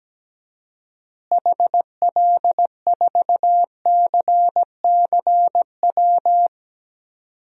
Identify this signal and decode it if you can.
Morse code, heard as a keyed tone, about 17 wpm, 700 Hz. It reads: HL4CCW
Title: DXCC-100 Call Signs - 6 Characters - Repeated 3x - 17wpm